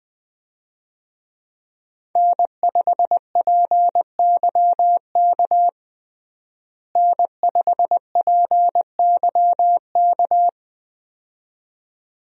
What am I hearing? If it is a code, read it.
Morse code, heard as a keyed tone, about 21 wpm, 700 Hz. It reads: N5PYK N5PYK
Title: USA Call Signs - 5 Characters - Repeated 2x - 20wpm